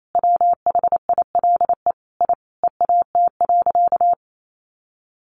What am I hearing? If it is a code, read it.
Morse code, heard as a keyed tone, about 28 wpm, 700 Hz. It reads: WHILE I EAT.
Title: Sentences from Top 1500 Words 28wpm